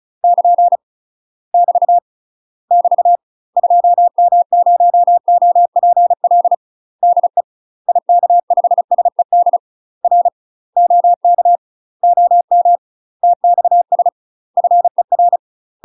Morse code, heard as a keyed tone, about 36 wpm, 700 Hz. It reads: <KN> <BT> <BT> 2M0OPL DE IK5SED R OK OM TXS FER
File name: UK MASTER QSOs TRAINING / 35WPM / UK MASTER QSOs 0031